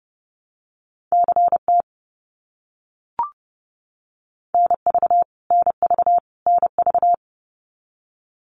Morse code, heard as a keyed tone, about 30 wpm, 700 Hz. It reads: CT NV NV NV
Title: US States Repeated 3x 30wpm